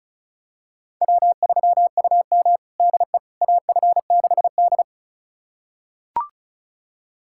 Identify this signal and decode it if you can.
Morse code, heard as a keyed tone, about 36 wpm, 700 Hz. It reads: W3UM DE AF6D
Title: USA Callsign DE Callsign - 4 Characters - 35wpm